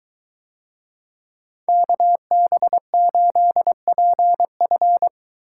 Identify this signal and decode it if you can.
Morse code, heard as a keyed tone, about 20 wpm, 700 Hz. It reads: KB8PF